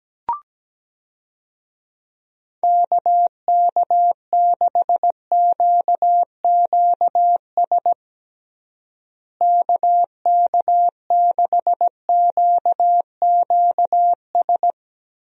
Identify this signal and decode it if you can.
Morse code, heard as a keyed tone, about 17 wpm, 700 Hz. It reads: KK6QQS KK6QQS
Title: USA Call Signs - 6 Characters - Repeated 2x - 17wpm